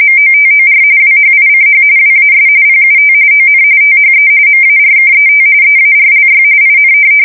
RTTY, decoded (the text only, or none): MOTOR WHICH WAS PUT ON THE MARKET EARLY IN